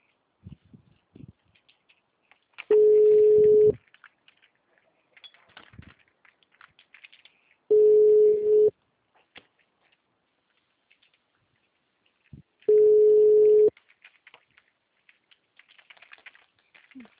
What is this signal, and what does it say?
Ringback tone (Europe, 425 Hz).